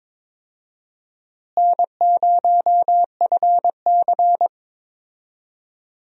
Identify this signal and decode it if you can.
Morse code, heard as a keyed tone, about 22 wpm, 700 Hz. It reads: N0FC
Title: USA Call Signs - 4 Characters - Repeated 2x - 22wpm